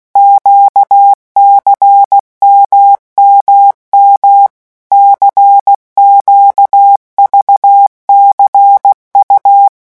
Morse code, heard as a keyed tone, about 16 wpm, 800 Hz. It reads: QCMMM CQVCU
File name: morse mp3 / revision2-vitesse 15 mots